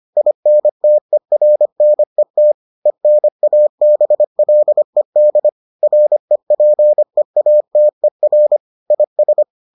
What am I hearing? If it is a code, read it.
Morse code, heard as a keyed tone, about 25 wpm, 600 Hz. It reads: INTERNET ENABLED REPEATER IS